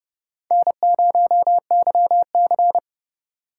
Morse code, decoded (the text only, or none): N0YC